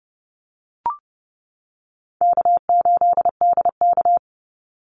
Morse code, heard as a keyed tone, about 30 wpm, 700 Hz. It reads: K8DK
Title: Call Signs - 30wpm